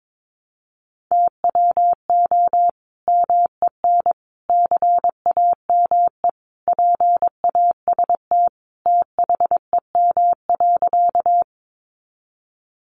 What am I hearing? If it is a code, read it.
Morse code, heard as a keyed tone, about 22 wpm, 700 Hz. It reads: TWO MEN CAME PAST THEM.
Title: Sentences from Top 600 Words 22wpm